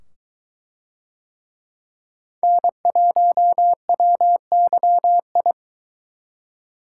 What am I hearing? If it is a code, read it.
Morse code, heard as a keyed tone, about 22 wpm, 700 Hz. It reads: N1WYI